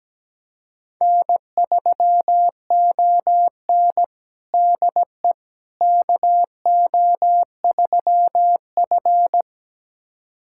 Morse code, heard as a keyed tone, about 17 wpm, 700 Hz. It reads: N3ON DE KO3F
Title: USA Callsign DE Callsign - 4 Characters - 17wpm